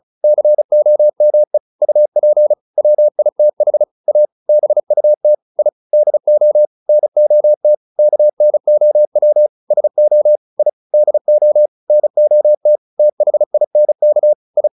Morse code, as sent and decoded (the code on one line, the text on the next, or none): -.-. --- -- . / ..- .--. / .-- .. - .... / .- / -... ..- - / .. / -.. --- / -. --- - / -.- -. --- .-- / ... --- / .. / -.. --- / -. --- - / - .... .. -. -.- / ..
COME UP WITH A BUT I DO NOT KNOW SO I DO NOT THINK I